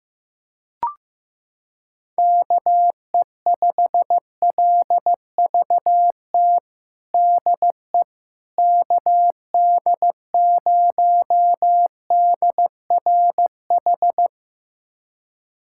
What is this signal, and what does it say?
Morse code, heard as a keyed tone, about 15 wpm, 700 Hz. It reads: KE5LVT DE KD0DRH
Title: USA Callsign DE Callsign - 6 Characters - 15wpm